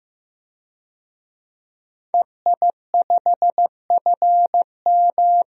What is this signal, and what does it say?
Morse code, heard as a keyed tone, about 16 wpm, 700 Hz. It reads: EI5FM
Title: DXCC-100 Call Signs - 5 Characters - Repeated 2x - 15wpm